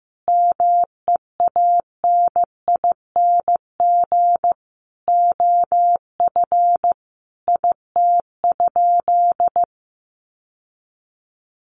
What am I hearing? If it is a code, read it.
Morse code, heard as a keyed tone, about 15 wpm, 700 Hz. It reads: MEANING OF IT?
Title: Sentences from Top 900 Words 15wpm